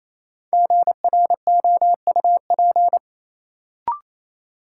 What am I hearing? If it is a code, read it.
Morse code, heard as a keyed tone, about 28 wpm, 700 Hz. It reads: GROUP
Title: Sentences from Top 1000 Words - Review Part 1 - New Word Drill - 28wpm